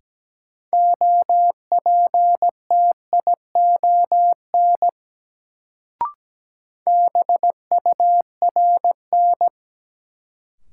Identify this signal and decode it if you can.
Morse code, heard as a keyed tone, about 17 wpm, 700 Hz. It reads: OPTION BURN
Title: Sentences from Top 1300 Words - New Word Drill Alt - 17wpm